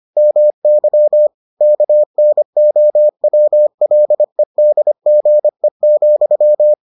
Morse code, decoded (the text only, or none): MY KNOWLEDGE,